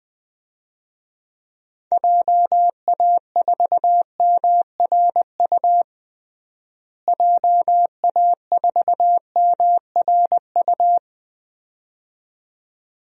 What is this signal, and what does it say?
Morse code, heard as a keyed tone, about 21 wpm, 700 Hz. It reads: JA4MRU JA4MRU
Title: DXCC-100 Call Signs - 6 Characters - Repeated 2x - 20wpm